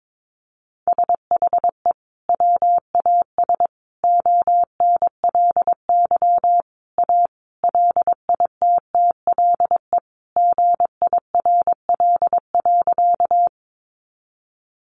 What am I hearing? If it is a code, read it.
Morse code, heard as a keyed tone, about 22 wpm, 700 Hz. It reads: SHE WAS ONLY A LITTLE GIRL.
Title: Sentences from Top 400 Words 22wpm